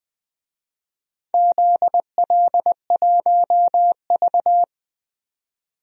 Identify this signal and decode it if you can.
Morse code, heard as a keyed tone, about 20 wpm, 700 Hz. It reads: ZL1V
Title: DXCC-100 Call Signs - 4 Characters - 20wpm